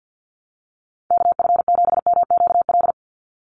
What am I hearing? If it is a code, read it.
Morse code, heard as a keyed tone, about 50 wpm, 700 Hz. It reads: KF7MQL
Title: USA Call Signs 50wpm